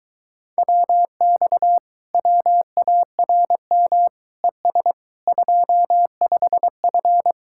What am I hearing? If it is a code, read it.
Morse code, heard as a keyed tone, about 22 wpm, 700 Hz. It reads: WX WARM ES 25F